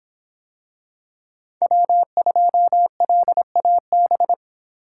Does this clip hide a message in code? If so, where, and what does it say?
Yes: Morse code from 1.6 to 4.3 s, heard as a keyed tone, about 25 wpm, 700 Hz. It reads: W2LAB